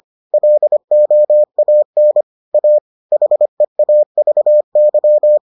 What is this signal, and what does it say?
Morse code, heard as a keyed tone, about 25 wpm, 600 Hz. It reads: LOAN A HEAVY